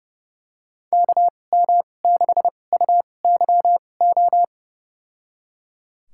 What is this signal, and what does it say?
Morse code, heard as a keyed tone, about 30 wpm, spaced as for 22 wpm, 700 Hz. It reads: KM6UYO
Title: USA Call Signs - 2x Character Spacing - 30wpm